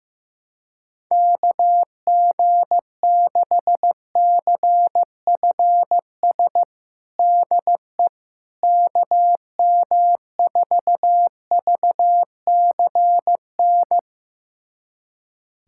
Morse code, heard as a keyed tone, about 15 wpm, 700 Hz. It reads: KG6CFS DE KM4VCN
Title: USA Callsign DE Callsign - 6 Characters - 15wpm